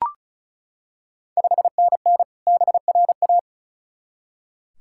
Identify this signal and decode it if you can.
Morse code, heard as a keyed tone, about 36 wpm, 690 Hz. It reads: 5NN BRA
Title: Ontario QSO Party Exchange 35wpm